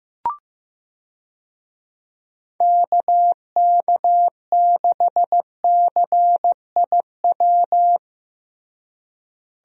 Morse code, heard as a keyed tone, about 15 wpm, 700 Hz. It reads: KK6CIW
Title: USA Call Signs - 6 Characters - Repeated 2x - 15wpm